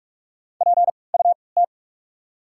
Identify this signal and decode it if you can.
Morse code, heard as a keyed tone, about 46 wpm, spaced as for 26 wpm, 690 Hz. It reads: PUT
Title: 3x Character Spacing - Top 200 Words - 45wpm